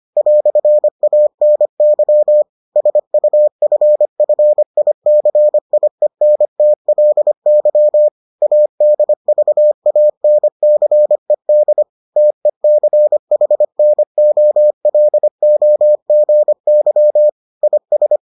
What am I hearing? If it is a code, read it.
Morse code, heard as a keyed tone, about 25 wpm, 600 Hz. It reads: "ANY SUFFICIENTLY ADVANCED TECHNOLOGY IS